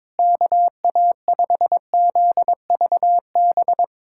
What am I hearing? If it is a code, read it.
Morse code, heard as a keyed tone, about 22 wpm, 700 Hz. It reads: KA5ZVB
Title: USA Callsign DE Callsign 22wpm